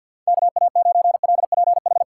Morse code, heard as a keyed tone, about 53 wpm, 700 Hz. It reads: KA9LPH
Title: USA Call Signs - 6 Characters - 4x Character Spacing - 50wpm